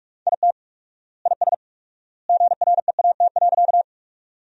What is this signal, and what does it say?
Morse code, heard as a keyed tone, about 42 wpm, 700 Hz. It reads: IT IS GREAT.
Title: 4x Word-Spacing - Sentences from Top 200 Words - 45wpm